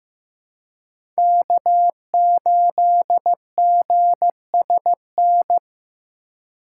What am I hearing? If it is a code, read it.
Morse code, heard as a keyed tone, about 15 wpm, 700 Hz. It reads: K8GSN
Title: USA Call Signs - 5 Characters - 3x Character Spacing - 15wpm